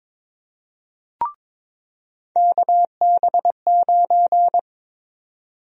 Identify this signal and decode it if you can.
Morse code, heard as a keyed tone, about 22 wpm, 700 Hz. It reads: KB9
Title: USA Call Sign Prefixes - 22wpm